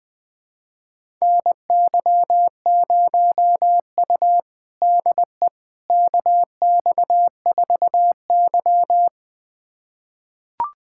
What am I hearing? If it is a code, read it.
Morse code, heard as a keyed tone, about 20 wpm, 700 Hz. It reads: NY0U DE KX4Y
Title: USA Callsign DE Callsign - 4 Characters - 20wpm